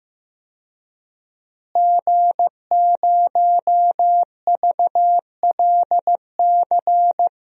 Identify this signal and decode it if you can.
Morse code, heard as a keyed tone, about 15 wpm, 700 Hz. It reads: G0VLC
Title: DXCC-100 Call Signs - 5 Characters - Repeated 3x - 15wpm